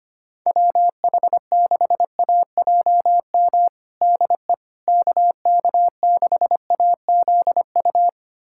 Morse code, heard as a keyed tone, about 25 wpm, 700 Hz. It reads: WH6AJM DE KK6AZU
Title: USA Callsign DE Callsign 25wpm